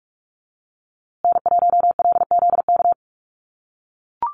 Morse code, heard as a keyed tone, about 44 wpm, 700 Hz. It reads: N1LZK
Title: USA Call Signs Repeated 2x 45wpm